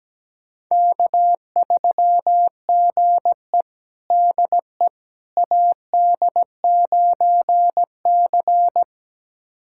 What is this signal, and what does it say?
Morse code, heard as a keyed tone, about 17 wpm, 700 Hz. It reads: K3GE DE AD9C
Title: USA Callsign DE Callsign - 4 Characters - 17wpm